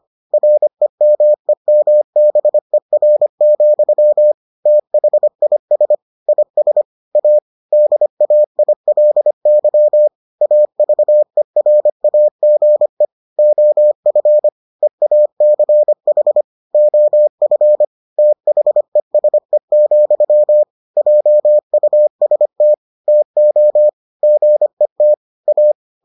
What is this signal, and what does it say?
Morse code, heard as a keyed tone, about 25 wpm, 600 Hz. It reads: REMEMBER, THIS IS A DAILY AVERAGE OF EACH OF THESE, JUST TO GET A